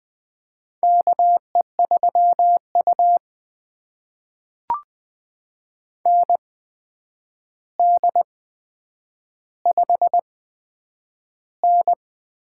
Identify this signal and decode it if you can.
Morse code, heard as a keyed tone, about 20 wpm, spaced as for 9 wpm, 700 Hz. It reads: KE3U ND5N
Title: USA Call Signs - 4 Characters - 8x Character Spacing - 20wpm